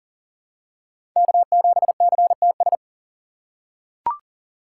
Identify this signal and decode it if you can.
Morse code, heard as a keyed tone, about 37 wpm, 700 Hz. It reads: K7CTS
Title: USA Call Signs - 3x Character Spacing - 40wpm